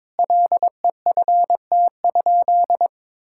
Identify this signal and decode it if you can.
Morse code, heard as a keyed tone, about 22 wpm, 700 Hz. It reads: LEFT?
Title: Sentences from Top 1000 Words - Review Part 6 - 22wpm